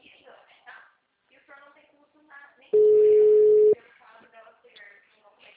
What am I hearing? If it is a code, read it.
Ringback tone (Europe, 425 Hz), with speech.